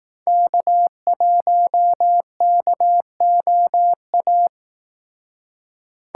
Morse code, heard as a keyed tone, about 19 wpm, 700 Hz. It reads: K1KOA
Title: Speed Racing - USA 1x3 Call Signs 15wpm